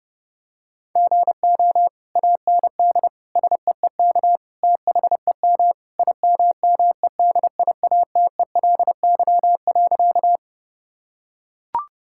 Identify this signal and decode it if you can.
Morse code, heard as a keyed tone, about 28 wpm, 700 Hz. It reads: GO AND SEEK THEM IMMEDIATELY.
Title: Sentences from Top 1600 Words 30wpm